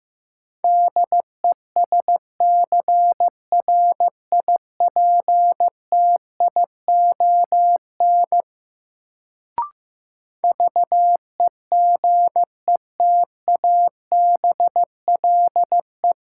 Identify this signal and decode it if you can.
Morse code, heard as a keyed tone, about 15 wpm, 700 Hz. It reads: DESCRIPTION VEGETABLE
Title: Sentences from Top 1300 Words - New Word Drill - 15wpm